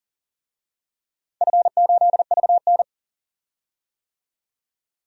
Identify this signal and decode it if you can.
Morse code, heard as a keyed tone, about 40 wpm, 700 Hz. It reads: F8VN
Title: DXCC-100 Call Signs - 4 Characters - 40wpm